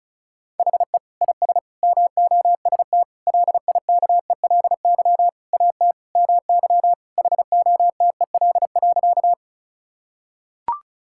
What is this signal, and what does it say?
Morse code, heard as a keyed tone, about 36 wpm, 700 Hz. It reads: HE IS MOST LIKELY AT MY HOTEL.
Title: Sentences from Top 1600 Words 35wpm